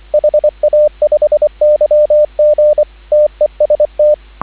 Morse code, heard as a keyed tone, about 24 wpm, 600 Hz, with radio noise. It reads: HA5YG TEST